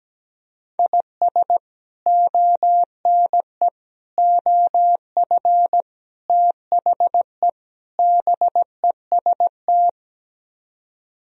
Morse code, heard as a keyed tone, about 17 wpm, 700 Hz. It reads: IS ONE OF THE BEST
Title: Speed-Racing - Sets of 5 Words 17wpm